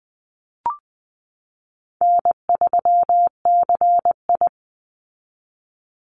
Morse code, heard as a keyed tone, about 20 wpm, 700 Hz. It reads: N3CI
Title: Easy Call Signs Repeated 2x - 20wpm